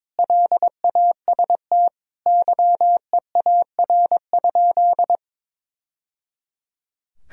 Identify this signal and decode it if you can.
Morse code, heard as a keyed tone, about 22 wpm, 700 Hz. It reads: LAST YEAR?
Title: Sentences from Top 1000 Words - Review Part 4 - 22wpm